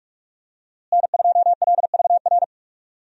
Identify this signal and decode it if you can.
Morse code, heard as a keyed tone, about 46 wpm, 700 Hz. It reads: N2LVR
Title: USA Call Signs - 5 Characters - Repeated 3x - 45wpm